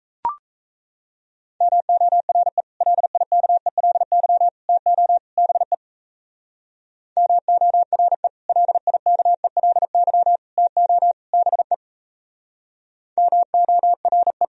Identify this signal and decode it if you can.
Morse code, heard as a keyed tone, about 34 wpm, 700 Hz. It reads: MORE LIKELY TO BE MORE LIKELY TO BE MORE
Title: Speed-Racing - Sets of 4 Words 28wpm